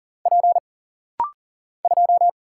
Morse code, heard as a keyed tone, about 39 wpm, 700 Hz. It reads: P 2
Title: Single Letter-Number - Rapid-Fire 40wpm